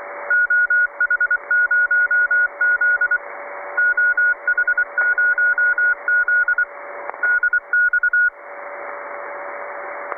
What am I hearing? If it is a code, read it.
Morse code, heard as a keyed tone, about 24 wpm, 1440 Hz. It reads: OH0Z OH0Z DX
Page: 1821 kHz (1822 LSB, 1,8kHzBandbreite, 8m Vertikal)